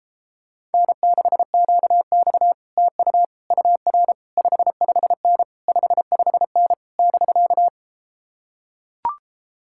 Morse code, heard as a keyed tone, about 33 wpm, 700 Hz. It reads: N6QX TU UR 55N 55N <BK>